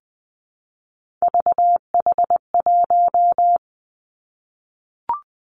Morse code, heard as a keyed tone, about 20 wpm, 700 Hz. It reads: VH1
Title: Three Challenging Letter-Number Combinations 20wpm